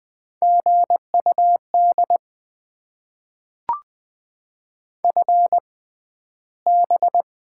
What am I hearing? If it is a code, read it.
Morse code, heard as a keyed tone, about 20 wpm, spaced as for 12 wpm, 700 Hz. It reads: GUD FB
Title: Top 20 QSO Elements - 6x Character Spacing - 20wpm